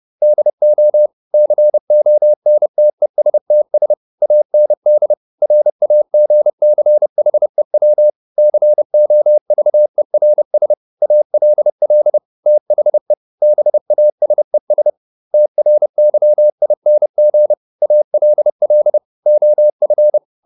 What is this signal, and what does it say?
Morse code, heard as a keyed tone, about 30 wpm, 600 Hz. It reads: DO CONTESTS AND RAGCHEW COVERS ALL THE BASES TRYING ALL OF